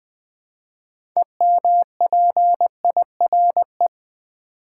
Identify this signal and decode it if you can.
Morse code, heard as a keyed tone, about 20 wpm, 700 Hz. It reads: EMPIRE
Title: Build A Word - 2 Letters At A Time - 6-Letter Words - 20wpm